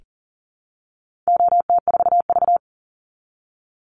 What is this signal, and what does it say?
Morse code, heard as a keyed tone, about 40 wpm, 700 Hz. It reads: OT4V
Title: Call Signs Repeated 2x - 40wpm